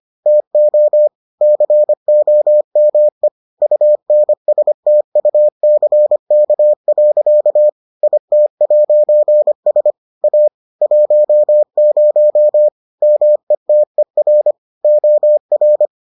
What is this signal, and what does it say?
Morse code, heard as a keyed tone, about 25 wpm, 600 Hz. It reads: TO COME UNSTUCK. IT'S A 10 METER OR